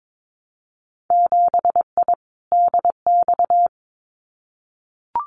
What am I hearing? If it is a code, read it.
Morse code, heard as a keyed tone, about 22 wpm, 700 Hz. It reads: 7I DX
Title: Winter Field Day Exchanges Encore 22wpm